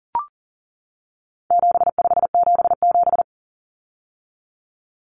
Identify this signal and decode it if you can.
Morse code, heard as a keyed tone, about 40 wpm, 700 Hz. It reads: 7577
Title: Common Cut Numbers - 4-Digit 40wpm